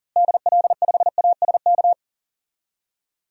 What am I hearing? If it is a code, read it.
Morse code, heard as a keyed tone, about 40 wpm, 700 Hz. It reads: DL5ASK
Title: DXCC-100 Call Signs - 6 Characters - Repeated 3x - 40wpm